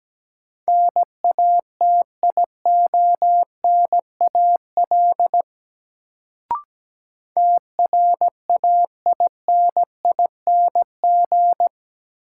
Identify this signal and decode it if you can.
Morse code, heard as a keyed tone, about 17 wpm, 700 Hz. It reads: NATIONAL TRAINING
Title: Sentences from Top 1000 Words - Review Part 4 - New Word Drill - 17wpm